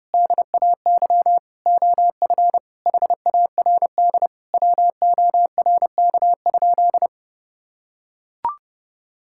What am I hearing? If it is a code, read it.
Morse code, heard as a keyed tone, about 30 wpm, 700 Hz. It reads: DAY OF HARD WORK?
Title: Sentences from Top 1000 Words - Review Part 6 - 30wpm